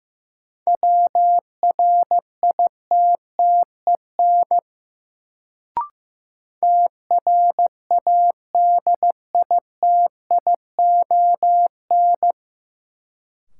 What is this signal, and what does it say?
Morse code, heard as a keyed tone, about 15 wpm, 700 Hz. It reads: WRITTEN TRADITION
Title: Sentences from Top 1500 Words - New Word Drill Alt - 15wpm